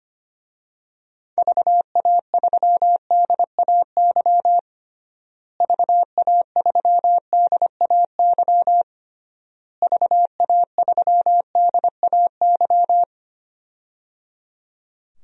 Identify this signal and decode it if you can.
Morse code, heard as a keyed tone, about 25 wpm, 700 Hz. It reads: VA3DAY VA3DAY VA3DAY
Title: DXCC-100 Call Signs - 6 Characters - Repeated 3x - 25wpm